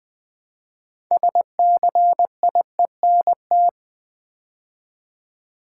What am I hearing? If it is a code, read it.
Morse code, heard as a keyed tone, about 20 wpm, 700 Hz. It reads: SCIENT
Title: Build A Word - 3 Letters At A Time - 9-Letter Words - 20wpm